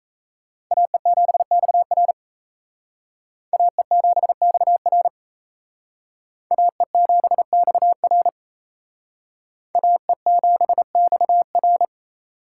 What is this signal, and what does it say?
Morse code, heard as a keyed tone, about 35 wpm, 700 Hz. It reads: AE7XR AE7XR AE7XR AE7XR
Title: Speed Racing - USA 2x2 Call Signs 28wpm